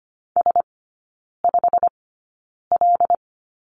Morse code, heard as a keyed tone, about 25 wpm, spaced as for 9 wpm, 690 Hz. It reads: S5L
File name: Lesson 15e - Number 5 - Three Characters 25wpm